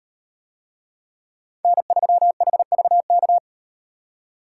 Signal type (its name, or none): Morse code